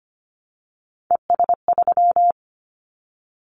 Morse code, heard as a keyed tone, about 25 wpm, 700 Hz. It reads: ES3
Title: Callsign Prefixes for Top 50 DXCC - 25wpm